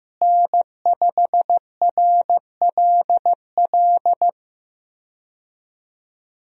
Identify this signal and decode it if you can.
Morse code, heard as a keyed tone, about 15 wpm, 700 Hz. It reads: N5RLL
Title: USA Call Signs - 5 Characters - Repeated 2x - 15wpm